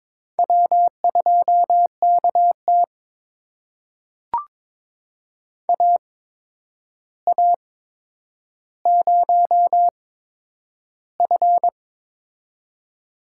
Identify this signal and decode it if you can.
Morse code, heard as a keyed tone, about 22 wpm, spaced as for 9 wpm, 700 Hz. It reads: W2KT AA0F
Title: USA Call Signs - 4 Characters - 8x Character Spacing - 22wpm